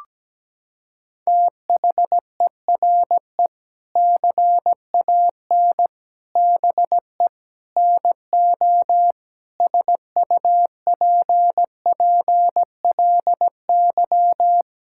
Morse code, heard as a keyed tone, about 17 wpm, 700 Hz. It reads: THERE CAN BE NO SUPPLY
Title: Sentences from Top 1000 Words - Review Part 5 - 17wpm